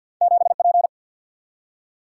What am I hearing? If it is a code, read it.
Morse code, heard as a keyed tone, about 53 wpm, 700 Hz. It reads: 7P
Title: Single-Letter Single-Number 50wpm